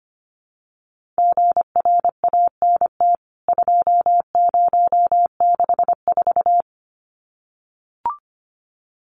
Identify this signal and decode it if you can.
Morse code, heard as a keyed tone, about 25 wpm, 700 Hz. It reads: GRANT 2064
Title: CWT Exchange 25wpm